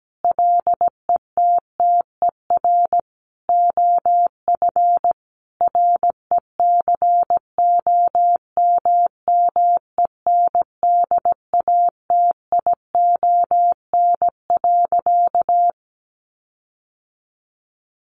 Morse code, heard as a keyed tone, about 17 wpm, 700 Hz. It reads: LETTER OF RECOMMENDATION.
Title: Sentences from Top 1400 Words 17wpm